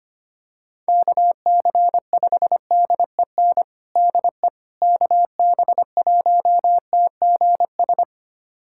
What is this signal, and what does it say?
Morse code, heard as a keyed tone, about 25 wpm, 700 Hz. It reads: KC5DEN DE KB1TGS
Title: USA Callsign DE Callsign - 6 Characters - 25wpm